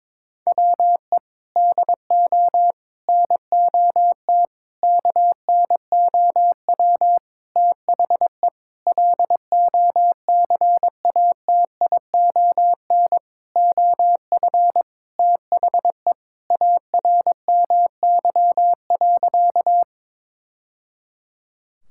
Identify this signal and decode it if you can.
Morse code, heard as a keyed tone, about 22 wpm, 700 Hz. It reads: WE DO NOT KNOW THE LOCATION OF THE ARMY.
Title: Sentences from Top 1000 Words - Review Part 6 - 22wpm